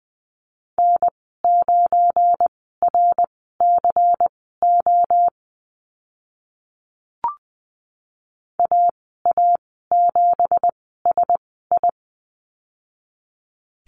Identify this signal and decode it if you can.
Morse code, heard as a keyed tone, about 20 wpm, spaced as for 15 wpm, 700 Hz. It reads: N9RCO AA7SI
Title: 186 Lesson 30g - Letter b - Call Signs 20wpm